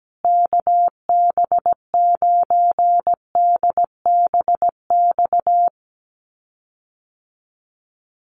Morse code, heard as a keyed tone, about 17 wpm, 700 Hz. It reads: KB9DBX
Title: USA Call Signs Repeated 3x 17wpm